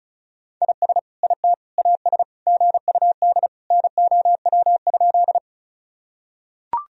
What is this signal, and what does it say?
Morse code, heard as a keyed tone, about 36 wpm, 700 Hz. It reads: IS IT AS GUD NOW?
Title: Sentences from Top 200 Words and Abbreviations 35wpm